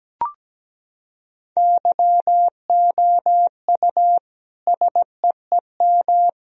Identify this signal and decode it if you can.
Morse code, heard as a keyed tone, about 17 wpm, 700 Hz. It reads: YOU SEEM
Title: Sentences from Top 1000 Words - Review Part 1 - 17wpm